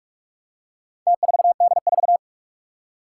Morse code, heard as a keyed tone, about 44 wpm, 700 Hz. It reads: T4D4
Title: Four Challenging Letter-Number Combinations 45wpm